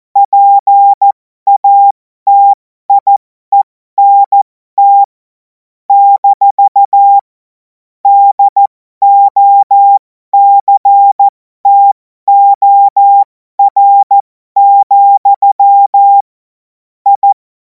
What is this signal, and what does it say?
Morse code, heard as a keyed tone, about 14 wpm, spaced as for 12 wpm, 800 Hz. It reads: PATIENT - DOCTOR, I
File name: Morsekurs / Witze